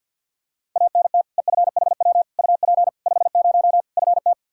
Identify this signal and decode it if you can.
Morse code, heard as a keyed tone, about 50 wpm, 700 Hz. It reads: ANT EFHW UP 50 FT
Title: K1USN SST QSO Practice 50wpm